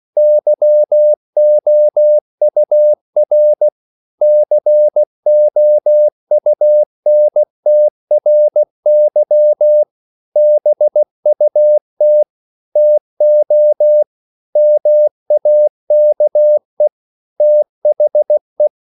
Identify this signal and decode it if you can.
Morse code, heard as a keyed tone, about 16 wpm, 600 Hz. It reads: YOUR COUNTRY BUT TO MAKE THE